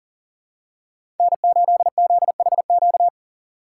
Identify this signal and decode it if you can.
Morse code, heard as a keyed tone, about 40 wpm, 700 Hz. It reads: N8ZHQ
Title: USA Call Signs - 5 Characters - 3x Character Spacing - 40wpm